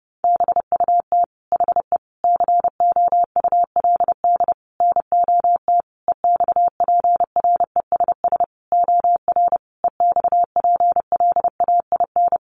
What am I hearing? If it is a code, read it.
Morse code, heard as a keyed tone, about 30 wpm, 700 Hz. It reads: BUT HE COULD NOT EXPRESS OR EXPLAIN
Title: Sentences from Top 1100 Words 30wpm